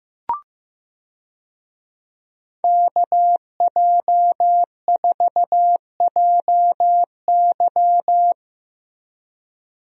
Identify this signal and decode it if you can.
Morse code, heard as a keyed tone, about 15 wpm, 700 Hz. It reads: KJ4JY
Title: USA Call Signs - 5 Characters - Repeated 2x - 15wpm